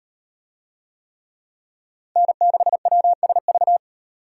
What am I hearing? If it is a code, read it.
Morse code, heard as a keyed tone, about 37 wpm, 700 Hz. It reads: N6WSV